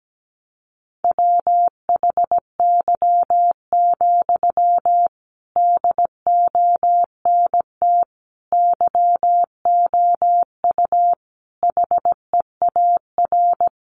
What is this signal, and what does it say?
Morse code, heard as a keyed tone, about 17 wpm, 700 Hz. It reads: WHY, DONT YOU HEAR
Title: Sentences from Top 300 Words 17wpm